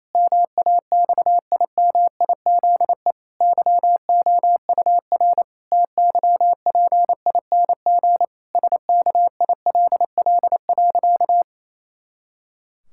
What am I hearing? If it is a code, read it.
Morse code, heard as a keyed tone, about 28 wpm, 700 Hz. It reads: MAXIMIZE YOUR TYPING SKILL.
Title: Pangram Challenge v2 - 28wpm